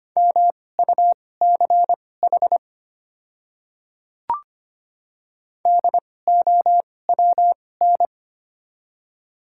Morse code, heard as a keyed tone, about 25 wpm, spaced as for 18 wpm, 700 Hz. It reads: MUCH DOWN
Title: 2x Character Spacing - Top 300 Words - 25wpm